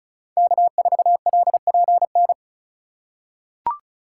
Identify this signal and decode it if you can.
Morse code, heard as a keyed tone, about 36 wpm, 700 Hz. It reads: K4LPN
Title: USA Call Signs - 2x Character Spacing - 35wpm